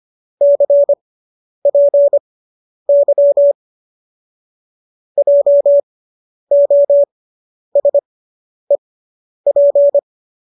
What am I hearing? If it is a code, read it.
Morse code, heard as a keyed tone, about 25 wpm, spaced as for 10 wpm, 570 Hz. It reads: CPY JOSEP